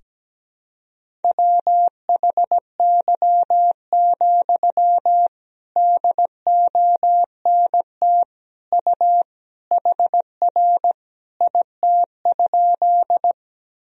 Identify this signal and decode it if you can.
Morse code, heard as a keyed tone, about 17 wpm, 700 Hz. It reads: WHY, DONT U HR IT?
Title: Sentences from Top 300 Words and Abbreviations 17wpm